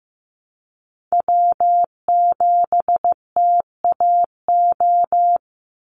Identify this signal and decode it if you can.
Morse code, heard as a keyed tone, about 15 wpm, 700 Hz. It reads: W7TAO
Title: NA SOTA Chaser Callsigns 15wpm